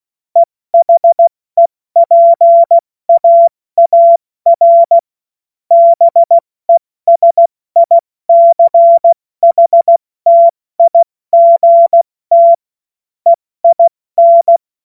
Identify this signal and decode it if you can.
Morse code, heard as a keyed tone, about 16 wpm, spaced as for 14 wpm, 680 Hz. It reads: EHEPAAR BESICHTIGT EIN
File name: Morsekurs / Witze / Witz 15. August 2023 70 F80